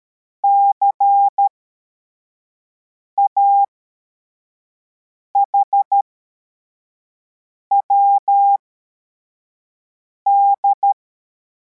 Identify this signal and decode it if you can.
Morse code, heard as a keyed tone, about 13 wpm, spaced as for 4 wpm, 800 Hz. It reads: CAHWD
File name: formation / cw / Text73